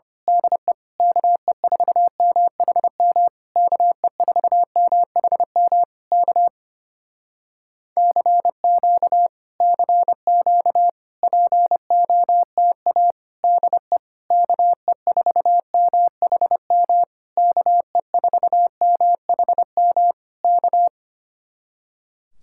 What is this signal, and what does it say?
Morse code, heard as a keyed tone, about 26 wpm, 700 Hz. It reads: DE KE4MHM KE4MHM K CQ CQ POTA DE KE4MHM KE4MHM K
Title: Speed-Racing Common QSO Phrases - 25wpm